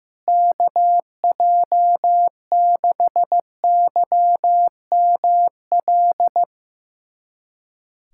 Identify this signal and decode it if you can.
Morse code, heard as a keyed tone, about 15 wpm, 700 Hz. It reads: KJ6YML
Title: USA Callsign DE Callsign - 6 Characters - 15wpm